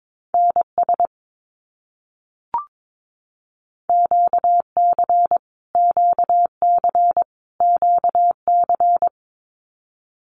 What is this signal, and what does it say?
Morse code, heard as a keyed tone, about 22 wpm, 700 Hz. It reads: NS QC QC QC
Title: Canadian Provinces and Territories - Repeated 3x - 22wpm